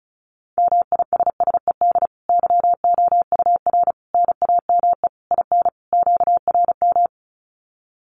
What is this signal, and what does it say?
Morse code, heard as a keyed tone, about 34 wpm, 700 Hz. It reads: MISSED YOUR NAME IN QRM
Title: Common QSO Phrases 35wpm